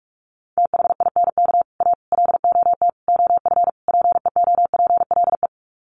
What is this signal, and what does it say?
Morse code, heard as a keyed tone, about 44 wpm, 700 Hz. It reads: THINK A LOT OF PEOPLE
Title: Sets of 5 Words 45wpm